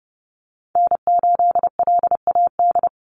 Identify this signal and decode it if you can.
Morse code, heard as a keyed tone, about 30 wpm, 700 Hz. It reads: N8LAD
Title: USA Call Signs 30wpm